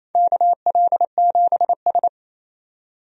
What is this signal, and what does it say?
Morse code, heard as a keyed tone, about 28 wpm, 700 Hz. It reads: KL7S
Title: DXCC-100 Call Signs - 4 Characters - 28wpm